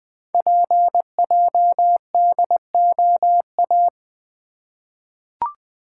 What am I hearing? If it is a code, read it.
Morse code, heard as a keyed tone, about 20 wpm, 700 Hz. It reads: PJDOA
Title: Code Groups - 5 Letters - 20wpm